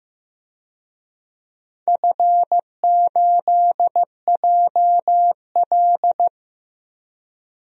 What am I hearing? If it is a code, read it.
Morse code, heard as a keyed tone, about 15 wpm, 700 Hz. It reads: F8JL
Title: DXCC-100 Call Signs - 4 Characters - 15wpm